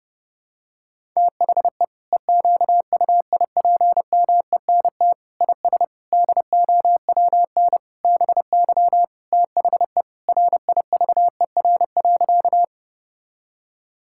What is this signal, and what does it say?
Morse code, heard as a keyed tone, about 30 wpm, 700 Hz. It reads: THE EQUIPMENT IS DOWN BY THE RIVER.
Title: Sentences from Top 1000 Words - Review Part 5 - 30wpm